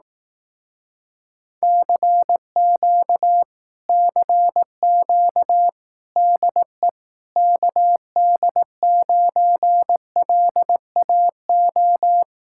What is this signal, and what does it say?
Morse code, heard as a keyed tone, about 18 wpm, 700 Hz. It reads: CQ CQ DE KD9LAO
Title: Speed-Racing Common QSO Phrases Encore - 15wpm